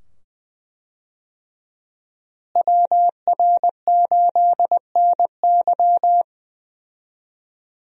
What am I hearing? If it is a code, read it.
Morse code, heard as a keyed tone, about 20 wpm, 700 Hz. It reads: WR8NY
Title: USA Call Signs - 5 Characters - Repeated 2x - 20wpm